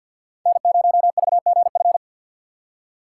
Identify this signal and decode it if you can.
Morse code, heard as a keyed tone, about 53 wpm, 700 Hz. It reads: N0VGF
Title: USA Call Signs - 5 Characters - 2x Character Spacing - 50wpm